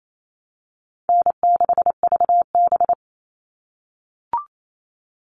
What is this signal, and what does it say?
Morse code, heard as a keyed tone, about 28 wpm, 700 Hz. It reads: N6VB
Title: All Cut Numbers - 4-Digit 28wpm